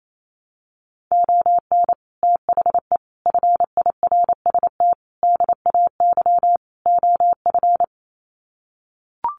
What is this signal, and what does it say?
Morse code, heard as a keyed tone, about 28 wpm, 700 Hz. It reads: ON THE FIRST DAY OF
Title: Sets of 5 Words Encore 28wpm